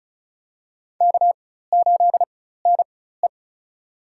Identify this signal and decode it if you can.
Morse code, heard as a keyed tone, about 36 wpm, spaced as for 16 wpm, 700 Hz. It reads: K8NE
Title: USA Call Signs - 4x Character Spacing - 35wpm